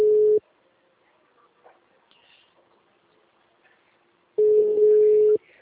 Ringback tone (Europe, 425 Hz).